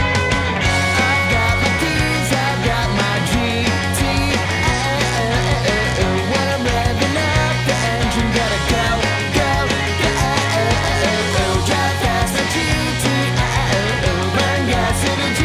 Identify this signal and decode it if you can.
Dial-up modem handshake.